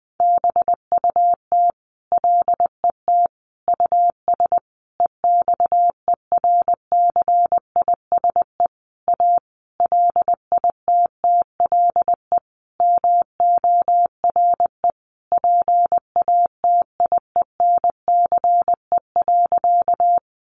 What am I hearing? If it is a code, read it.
Morse code, heard as a keyed tone, about 20 wpm, 700 Hz. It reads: BUT LET US EXERCISE A LITTLE MORE PATIENCE.
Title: Sentences from Top 1300 Words 20wpm